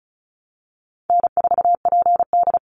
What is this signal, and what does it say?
Morse code, heard as a keyed tone, about 34 wpm, 700 Hz. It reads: N4PD
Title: Easy Call Signs Repeated 3x - 35wpm